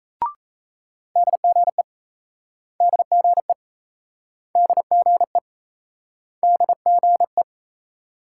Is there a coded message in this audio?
Yes: Morse code from 1.2 to 7.4 s, heard as a keyed tone, about 37 wpm, 700 Hz. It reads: DGE DGE DGE DGE